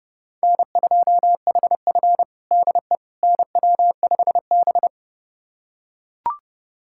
Morse code, heard as a keyed tone, about 30 wpm, 700 Hz. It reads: N2HF DE NW5B
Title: USA Callsign DE Callsign - 4 Characters - 30wpm